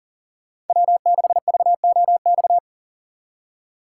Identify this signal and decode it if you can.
Morse code, heard as a keyed tone, about 48 wpm, 700 Hz. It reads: W6VOX
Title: Speed Racing - USA 1x3 Call Signs 40wpm